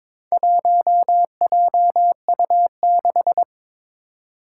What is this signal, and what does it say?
Morse code, heard as a keyed tone, about 22 wpm, 700 Hz. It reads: 1JU6
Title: Four Challenging Letter-Number Combinations 22wpm